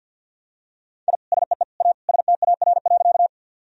Morse code, heard as a keyed tone, about 47 wpm, 700 Hz. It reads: I SEE A STAR.
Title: Sentences from Top 1000 Words - Review Part 2 - 50wpm